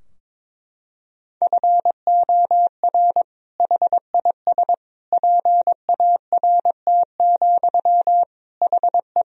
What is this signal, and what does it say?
Morse code, heard as a keyed tone, about 22 wpm, 700 Hz. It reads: FOR HIS PART, HE
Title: Sentences from Top 1000 Words - Review Part 2 - 22wpm